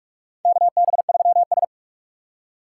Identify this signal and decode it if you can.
Morse code, heard as a keyed tone, about 46 wpm, 700 Hz. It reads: KB3S
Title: USA Call Signs - 4 Characters - 8x Character Spacing - 45wpm